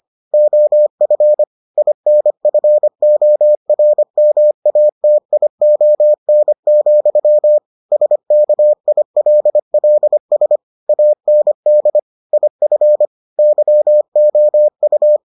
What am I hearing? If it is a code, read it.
Morse code, heard as a keyed tone, about 25 wpm, 600 Hz. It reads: OF INFORMATION, SKILLS AND IF YOU